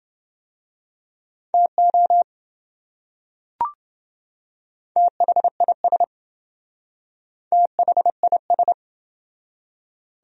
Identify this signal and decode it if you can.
Morse code, heard as a keyed tone, about 27 wpm, 700 Hz. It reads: TO THIS THIS